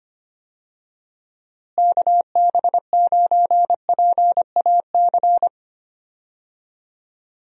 Morse code, heard as a keyed tone, about 25 wpm, 700 Hz. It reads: KB9PAC
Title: USA Call Signs - 6 Characters - Repeated 2x - 25wpm